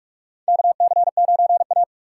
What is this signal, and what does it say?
Morse code, heard as a keyed tone, about 42 wpm, 700 Hz. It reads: KC9A
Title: USA Call Signs - 4 Characters - 8x Character Spacing - 45wpm